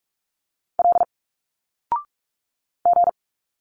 Morse code, heard as a keyed tone, about 46 wpm, 700 Hz. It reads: L G
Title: Single Letters 45wpm